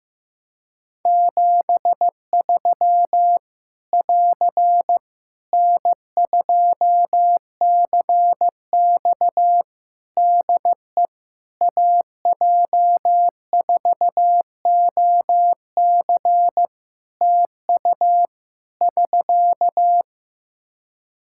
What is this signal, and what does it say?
Morse code, heard as a keyed tone, about 15 wpm, 700 Hz. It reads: 73 <AR> N2CX DE AJ4OC TU <SK>
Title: Practice QSO - Exchange 4 - Ending - 15wpm